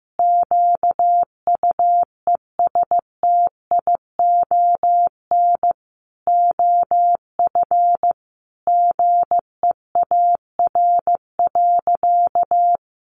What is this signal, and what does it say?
Morse code, heard as a keyed tone, about 15 wpm, 700 Hz. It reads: QUESTION OF GEAR.
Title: Sentences from Top 1400 Words 15wpm